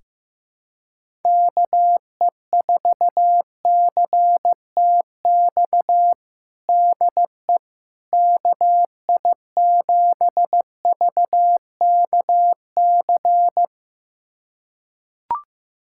Morse code, heard as a keyed tone, about 15 wpm, 700 Hz. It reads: KE4CTX DE KI7VKC
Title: USA Callsign DE Callsign - 6 Characters - 15wpm